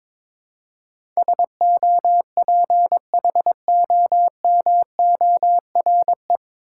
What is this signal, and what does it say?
Morse code, heard as a keyed tone, about 22 wpm, 700 Hz. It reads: SOPHOMORE
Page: Morse Code Ninja practice 22wpm